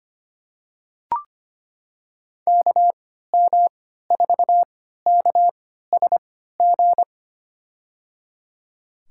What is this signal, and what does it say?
Morse code, heard as a keyed tone, about 25 wpm, spaced as for 14 wpm, 700 Hz. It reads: KM4KSG
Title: USA Call Signs - 6 Characters - 3x Character Spacing - 25wpm